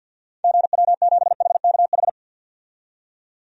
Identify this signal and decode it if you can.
Morse code, heard as a keyed tone, about 52 wpm, 700 Hz. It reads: GW7HKH
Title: DXCC-100 Call Signs - 6 Characters - 50wpm